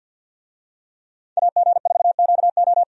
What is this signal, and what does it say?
Morse code, heard as a keyed tone, about 49 wpm, 700 Hz. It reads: AG4QO
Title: USA Call Signs - 5 Characters - 50wpm